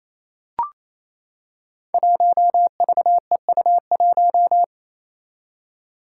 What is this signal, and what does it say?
Morse code, heard as a keyed tone, about 28 wpm, 700 Hz. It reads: 1VEU1
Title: Five Challenging Letter-Number Combinations 28wpm